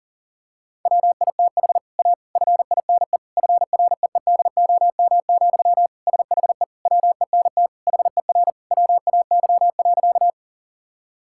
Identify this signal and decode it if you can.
Morse code, heard as a keyed tone, about 40 wpm, 700 Hz. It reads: WITH A FINE FREEDOM, SHE WENT HER WAY.
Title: Sentences from Top 1000 Words - Review Part 5 - 40wpm